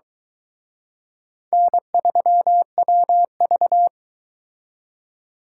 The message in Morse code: N3WV

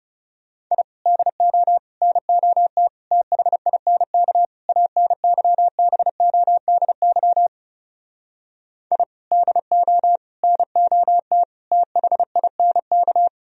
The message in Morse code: I DO NOT THINK ANYBODY I DO NOT THINK

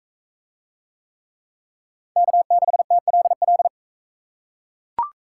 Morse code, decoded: K6TLL

